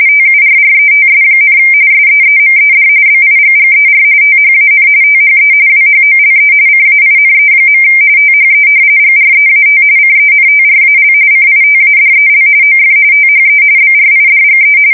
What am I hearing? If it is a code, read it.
RTTY, about 45.45 baud. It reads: TURRETS, SEEMED BIGGER THAN THE WHOLE CAINE. THE GREAT CENTRAL CITADEL OF THE BRIDGE
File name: fsk170 / battleshipnj